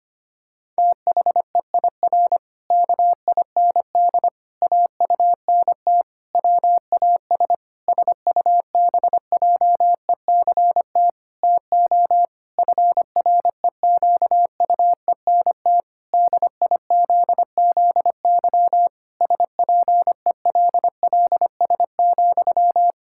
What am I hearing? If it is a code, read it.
Morse code, heard as a keyed tone, about 25 wpm, 700 Hz. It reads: THEIR KIND AUNT WAS SUBJECT TO FREQUENT DIZZY SPELLS,
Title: Pangram Challenge v2 - 25wpm